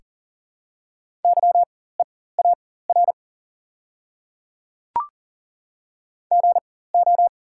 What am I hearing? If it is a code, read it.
Morse code, heard as a keyed tone, about 40 wpm, spaced as for 19 wpm, 700 Hz. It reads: YEAR GO
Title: 4x Character Spacing - Top 200 Words - 40wpm